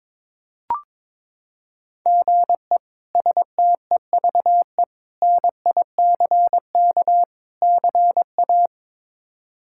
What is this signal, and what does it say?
Morse code, heard as a keyed tone, about 22 wpm, 700 Hz. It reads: GE STEVE NICK CA
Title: K1USN SST QSO Practice - 22wpm